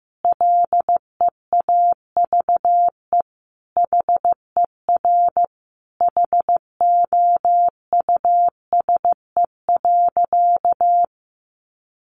Morse code, heard as a keyed tone, about 15 wpm, 700 Hz. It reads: LEAVE HER HOUSE.
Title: Sentences from Top 1500 Words 15wpm